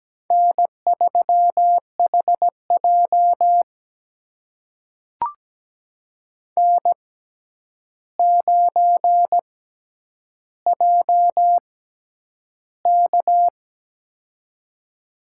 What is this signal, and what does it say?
Morse code, heard as a keyed tone, about 17 wpm, spaced as for 9 wpm, 700 Hz. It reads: N3HJ N9JK
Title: USA Call Signs - 4 Characters - 6x Character Spacing - 17wpm